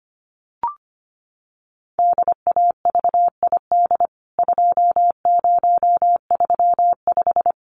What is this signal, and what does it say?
Morse code, heard as a keyed tone, about 25 wpm, 700 Hz. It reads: DAVID 2035
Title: CWT Exchange 25wpm